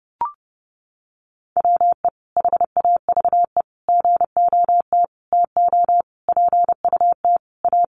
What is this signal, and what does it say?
Morse code, heard as a keyed tone, about 28 wpm, 700 Hz. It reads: WE HAVE GOT TO PUT A
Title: Sentences from Top 1200 Words 30wpm